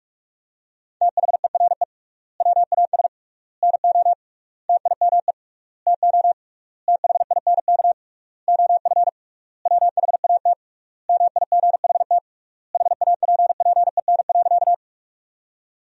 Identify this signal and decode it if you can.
Morse code, heard as a keyed tone, about 46 wpm, 700 Hz. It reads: THERE WAS NO TIME TO THINK OF WHAT MIGHT HAPPEN.
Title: 3x Word-Spacing - Sentences from Top 400 Words - 45wpm